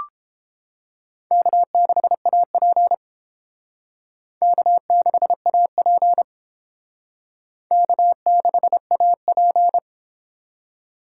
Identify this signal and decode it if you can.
Morse code, heard as a keyed tone, about 31 wpm, 700 Hz. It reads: K6AP K6AP K6AP